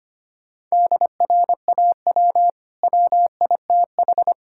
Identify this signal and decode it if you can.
Morse code, heard as a keyed tone, about 25 wpm, 700 Hz. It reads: DRAW WITH
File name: Easy Sentences with Hints - 25wpm